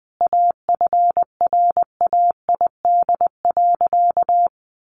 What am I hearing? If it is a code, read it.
Morse code, heard as a keyed tone, about 20 wpm, 700 Hz. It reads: AFRAID.
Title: Sentences from Top 1500 Words 20wpm